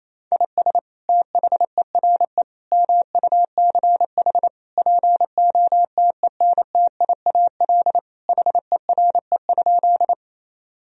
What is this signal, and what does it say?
Morse code, heard as a keyed tone, about 28 wpm, 700 Hz. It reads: IS THERE MUCH POTENTIAL HERE?
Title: Sentences from Top 1000 Words - Review Part 4 - 28wpm